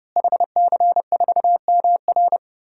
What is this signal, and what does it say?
Morse code, heard as a keyed tone, about 32 wpm, 700 Hz. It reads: HC4MR
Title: DXCC-100 Call Signs - 5 Characters - Repeated 2x - 30wpm